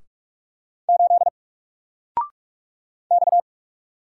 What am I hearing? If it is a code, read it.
Morse code, heard as a keyed tone, about 44 wpm, 700 Hz. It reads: 8 X
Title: Single Letter-Number Alt 45wpm